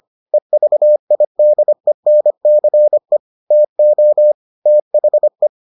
Morse code, heard as a keyed tone, about 25 wpm, 600 Hz. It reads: EVIDENCE TO THE